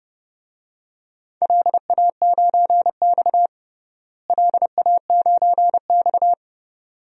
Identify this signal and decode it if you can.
Morse code, heard as a keyed tone, about 30 wpm, 700 Hz. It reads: LA9X LA9X
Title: DXCC-100 Call Signs - 4 Characters - Repeated 2x - 30wpm